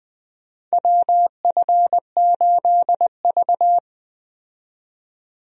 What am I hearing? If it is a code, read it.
Morse code, heard as a keyed tone, about 20 wpm, 700 Hz. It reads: WF8V
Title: USA Call Signs - 4 Characters - 20wpm